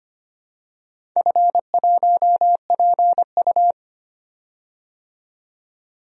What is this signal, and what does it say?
Morse code, heard as a keyed tone, about 25 wpm, 700 Hz. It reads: F1PU
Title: DXCC-100 Call Signs - 4 Characters - 25wpm